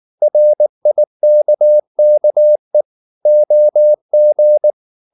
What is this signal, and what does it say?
Morse code, heard as a keyed tone, about 19 wpm, 600 Hz. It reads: RIKKE OG
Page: Ferie 19 wpm